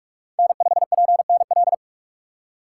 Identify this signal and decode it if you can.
Morse code, heard as a keyed tone, about 42 wpm, 700 Hz. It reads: N5PNL
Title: USA Call Signs - 5 Characters - 3x Character Spacing - 45wpm